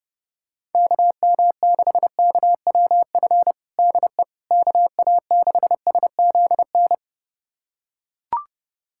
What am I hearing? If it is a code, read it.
Morse code, heard as a keyed tone, about 30 wpm, 700 Hz. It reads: KM6KWF DE KA6SZN
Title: USA Callsign DE Callsign - 6 Characters - 30wpm